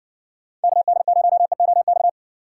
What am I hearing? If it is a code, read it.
Morse code, heard as a keyed tone, about 60 wpm, 690 Hz. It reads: KD0EOX